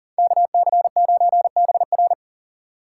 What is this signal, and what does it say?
Morse code, heard as a keyed tone, about 49 wpm, 700 Hz. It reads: KC9BR
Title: Speed Racing - USA 2x2 Call Signs 40wpm